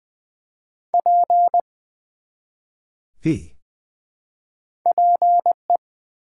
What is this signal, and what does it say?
Morse code, heard as a keyed tone, about 20 wpm, 700 Hz. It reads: P PE
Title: Build A Word - 1 Letter At A Time - 3-Letter Words - 20wpm